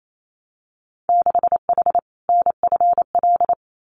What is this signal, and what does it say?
Morse code, heard as a keyed tone, about 28 wpm, 690 Hz. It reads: 6H NFL
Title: Winter Field Day Exchanges Encore 28wpm